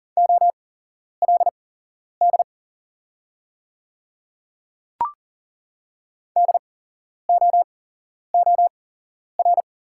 Morse code, heard as a keyed tone, about 40 wpm, spaced as for 11 wpm, 700 Hz. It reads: OLD DOOR
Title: 8x Character Spacing - Top 500 Words - 40wpm